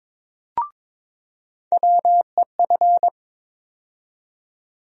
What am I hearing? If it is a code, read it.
Morse code, heard as a keyed tone, about 22 wpm, 700 Hz. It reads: WEF
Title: USA Call Sign Postfixes - 3 Letters 22wpm